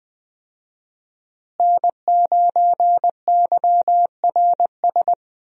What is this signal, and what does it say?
Morse code, heard as a keyed tone, about 20 wpm, 700 Hz. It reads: N9YRS
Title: USA Call Signs - 5 Characters - Repeated 3x - 20wpm